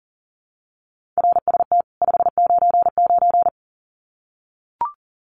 Morse code, heard as a keyed tone, about 40 wpm, 700 Hz. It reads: RST 599
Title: Common QSO Phrases 40wpm